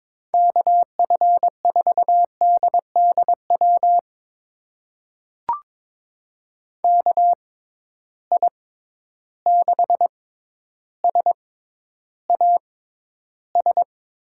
Morse code, heard as a keyed tone, about 22 wpm, spaced as for 11 wpm, 700 Hz. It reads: KF4DDW KI6SAS
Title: USA Call Signs - 6 Characters - 6x Character Spacing - 22wpm